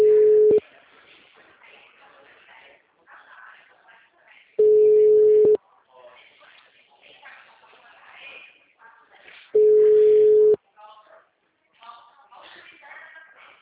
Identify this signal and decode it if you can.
Ringback tone (Europe, 425 Hz), with speech.
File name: out- -2017- - - .19629